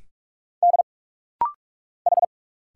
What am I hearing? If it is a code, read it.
Morse code, heard as a keyed tone, about 46 wpm, 700 Hz. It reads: D H
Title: Single Letters - Rapid-Fire 45wpm